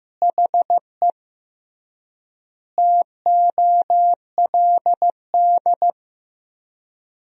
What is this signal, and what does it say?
Morse code, heard as a keyed tone, about 16 wpm, 700 Hz. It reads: HE TOLD
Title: 3x Word-Spacing - Sentences from Top 500 Words - 15wpm